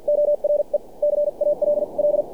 Morse code, heard as a keyed tone, about 47 wpm, spaced as for 40 wpm, 600 Hz. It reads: ORE CALL
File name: archived / CRSnet